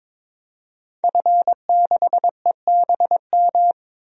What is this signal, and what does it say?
Morse code, heard as a keyed tone, about 22 wpm, 700 Hz. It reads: F6EBM
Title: DXCC-100 Call Signs - 5 Characters - 22wpm